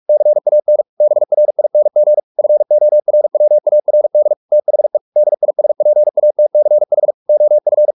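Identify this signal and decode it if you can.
Morse code, heard as a keyed tone, about 44 wpm, 600 Hz. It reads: CAN BRING FORWARD THE DISPATCH OF
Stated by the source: Morse practice at 45 wpm, the novel Never by Ken Follett in English, from CW med Gnister.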